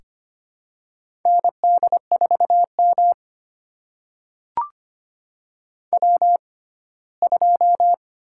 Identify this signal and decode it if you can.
Morse code, heard as a keyed tone, about 25 wpm, spaced as for 17 wpm, 700 Hz. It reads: ND4M W2
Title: USA Call Signs - 6x Character Spacing - 25wpm